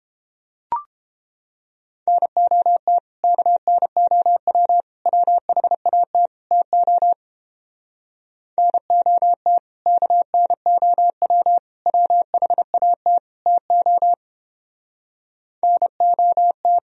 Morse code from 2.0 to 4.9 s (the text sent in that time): NOT KNOW